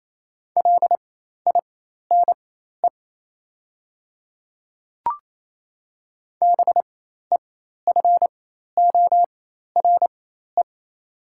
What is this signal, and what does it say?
Morse code, heard as a keyed tone, about 28 wpm, spaced as for 13 wpm, 700 Hz. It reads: LINE BEFORE
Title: 4x Character Spacing - Top 200 Words - 28wpm